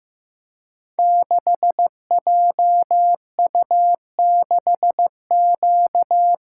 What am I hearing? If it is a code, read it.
Morse code, heard as a keyed tone, about 15 wpm, 700 Hz. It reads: 6JU6Q
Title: Code Groups - 5 Letter:Numbers - 15wpm